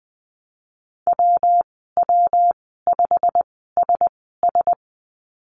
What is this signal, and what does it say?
Morse code, heard as a keyed tone, about 20 wpm, spaced as for 15 wpm, 700 Hz. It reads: WW5SS
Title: 238 Lesson 38g - Letter q - Call Signs 20wpm